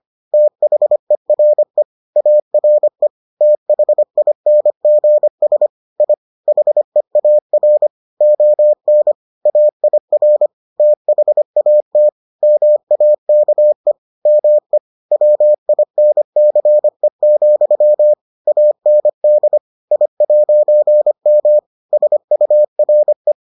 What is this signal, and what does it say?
Morse code, heard as a keyed tone, about 25 wpm, 600 Hz. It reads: THERE ARE THINGS I HEAR ON AIR THAT MAKE ME WINCE, AND I'M SURE